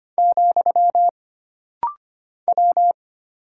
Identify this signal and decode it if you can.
Morse code, heard as a keyed tone, about 25 wpm, 700 Hz. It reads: , W
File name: Single Character (Letter:Number:Punctuation:Prosign) - Rapid-Fire - 25wpm